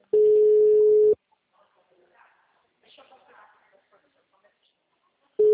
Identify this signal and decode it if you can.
Ringback tone (Europe, 425 Hz), with speech.